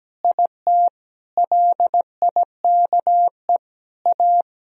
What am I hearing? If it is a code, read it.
Morse code, heard as a keyed tone, about 17 wpm, 700 Hz. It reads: IT LIKE A
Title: Sentences from Top 300 Words and Abbreviations 17wpm